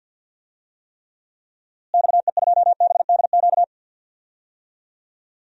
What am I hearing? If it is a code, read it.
Morse code, heard as a keyed tone, about 52 wpm, 700 Hz. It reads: XE2BDQ
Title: DXCC-100 Call Signs - 6 Characters - Repeated 2x - 50wpm